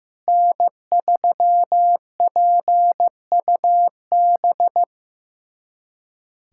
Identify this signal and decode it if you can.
Morse code, heard as a keyed tone, about 15 wpm, 700 Hz. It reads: N3PUB
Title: USA Callsign DE Callsign - 5 Characters - 15wpm